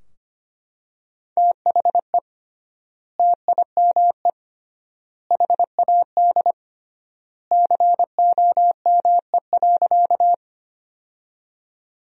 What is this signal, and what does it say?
Morse code, heard as a keyed tone, about 25 wpm, 700 Hz. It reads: THE TIME HAD COME.
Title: 3x Word-Spacing - Sentences from Top 100 Words - 25wpm